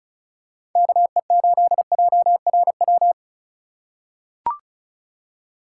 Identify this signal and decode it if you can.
Morse code, heard as a keyed tone, about 34 wpm, 700 Hz. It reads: KE8JRW
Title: USA Call Signs - 2x Character Spacing - 35wpm